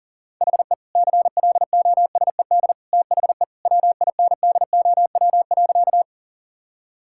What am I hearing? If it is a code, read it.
Morse code, heard as a keyed tone, about 40 wpm, 700 Hz. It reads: HE CLOSED THE WINDOW.
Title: Sentences from Top 1600 Words 40wpm